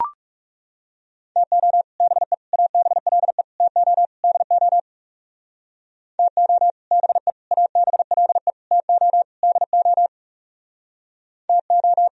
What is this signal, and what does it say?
Morse code, heard as a keyed tone, about 40 wpm, 700 Hz. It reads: TO BE ABLE TO DO TO BE ABLE TO DO TO